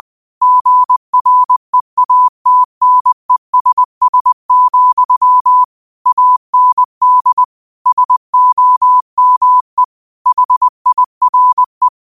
Morse code, heard as a keyed tone, about 20 wpm, 1000 Hz. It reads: GREATNESS, AND SOME HIRE
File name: qotdfiles / 20wpm